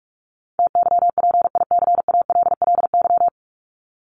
Morse code, heard as a keyed tone, about 44 wpm, 700 Hz. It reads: TYPICALLY
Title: LY Suffix - Top 100 Words with Base Words 45wpm